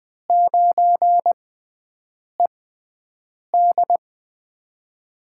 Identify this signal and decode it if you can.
Morse code, heard as a keyed tone, about 20 wpm, spaced as for 7 wpm, 700 Hz. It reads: 9ED
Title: USA Call Signs - 4 Characters - 6x Character Spacing - 20wpm